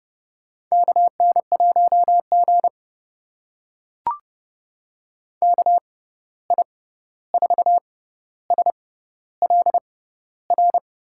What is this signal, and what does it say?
Morse code, heard as a keyed tone, about 30 wpm, spaced as for 14 wpm, 700 Hz. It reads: KN1G KI4SLR
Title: USA Call Signs - 6x Character Spacing - 30wpm